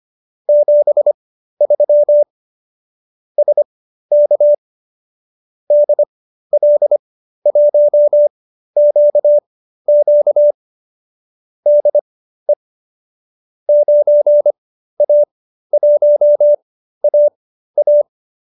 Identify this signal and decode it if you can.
Morse code, heard as a keyed tone, about 25 wpm, spaced as for 13 wpm, 590 Hz. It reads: 73 SK DL1QQ DE 9A1AA